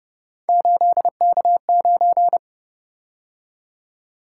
Morse code, heard as a keyed tone, about 29 wpm, 700 Hz. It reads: 8K9